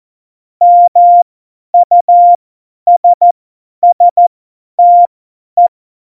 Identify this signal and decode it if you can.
Morse code, heard as a keyed tone, about 14 wpm, spaced as for 10 wpm, 700 Hz. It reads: MUSSTE
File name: Morsekurs / Witze / Witz 29. September 2020, 50 F70